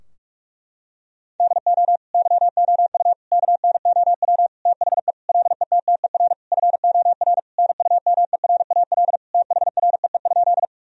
Morse code, heard as a keyed tone, about 43 wpm, 700 Hz. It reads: DO YOU KNOW THE LETTER FOR NUMERAL THREE?
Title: Sentences from Top 1000 Words - Review Part 2 - 45wpm